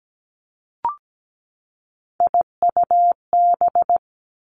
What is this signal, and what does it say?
Morse code, heard as a keyed tone, about 17 wpm, 700 Hz. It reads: IUB
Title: Three Challenging Letter-Number Combinations 17wpm